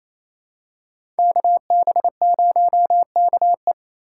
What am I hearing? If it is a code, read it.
Morse code, heard as a keyed tone, about 28 wpm, 700 Hz. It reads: KB0KE
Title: USA Call Signs - 3x Character Spacing - 28wpm